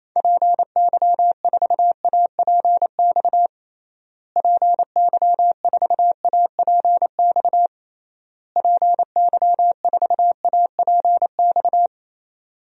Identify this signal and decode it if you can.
Morse code, heard as a keyed tone, about 28 wpm, 700 Hz. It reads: PY4APX PY4APX PY4APX
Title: DXCC-100 Call Signs - 6 Characters - Repeated 3x - 28wpm